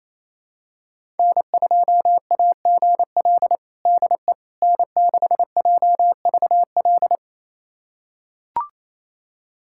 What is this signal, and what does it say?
Morse code, heard as a keyed tone, about 28 wpm, 700 Hz. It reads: N2AGL DE N6JVL
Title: USA Callsign DE Callsign - 5 Characters - 28wpm